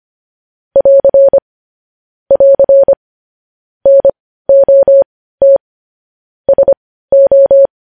Morse code, heard as a keyed tone, about 25 wpm, spaced as for 15 wpm, 550 Hz. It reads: <AR> <AR> NOT SO
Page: News Headlines in Morse Code at 15 WPM